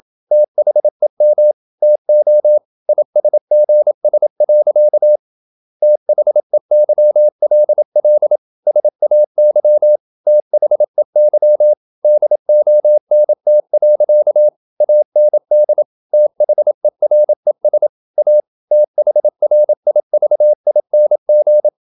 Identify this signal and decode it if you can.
Morse code, heard as a keyed tone, about 27 wpm, 600 Hz. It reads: THEM TO ISGS. THEYLL SAY THEY DONT. AND THERES A THRIVING
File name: Never 0027wpm